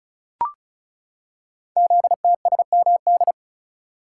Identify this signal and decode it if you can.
Morse code, heard as a keyed tone, about 36 wpm, 700 Hz. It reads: ZTSMD
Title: Code Groups - 5 Letters - 35wpm